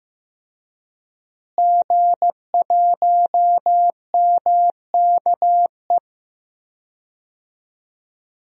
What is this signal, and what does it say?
Morse code, heard as a keyed tone, about 15 wpm, 700 Hz. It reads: G1MKE
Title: DXCC-100 Call Signs - 5 Characters - Repeated 3x - 15wpm